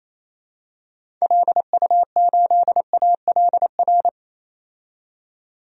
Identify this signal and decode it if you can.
Morse code, heard as a keyed tone, about 28 wpm, 700 Hz. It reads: LU8ALR
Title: DXCC-100 Call Signs - 6 Characters - 28wpm